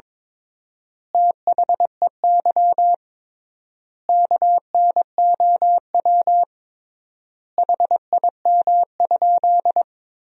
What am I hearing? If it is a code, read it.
Morse code, heard as a keyed tone, about 22 wpm, 700 Hz. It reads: THEY KNOW HIM?
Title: 3x Word-Spacing - Sentences from Top 100 Words - 22wpm